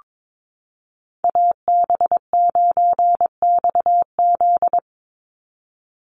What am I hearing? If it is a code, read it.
Morse code, heard as a keyed tone, about 22 wpm, 700 Hz. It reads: AB9XZ
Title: USA Call Signs 22wpm